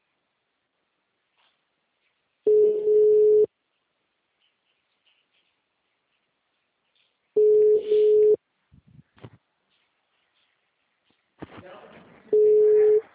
Ringback tone (Europe, 425 Hz).
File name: out- -2017- - - .18351